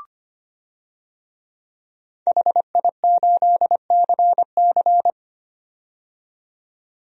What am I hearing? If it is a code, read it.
Morse code, heard as a keyed tone, about 25 wpm, 700 Hz. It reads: HI8CC
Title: DXCC-100 Call Signs - 5 Characters - 25wpm